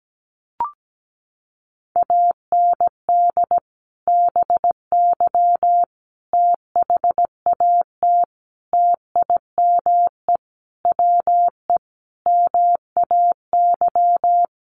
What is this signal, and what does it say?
Morse code, heard as a keyed tone, about 17 wpm, 700 Hz. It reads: AND BY THAT TIME WE MAY
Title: Sentences from Top 1100 Words 17wpm